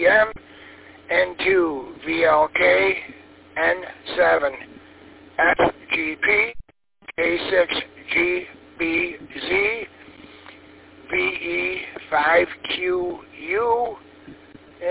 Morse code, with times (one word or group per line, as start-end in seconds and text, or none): none